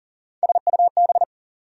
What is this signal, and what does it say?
Morse code, heard as a keyed tone, about 40 wpm, 700 Hz. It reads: SUB
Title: Prefixes and Suffixes 40wpm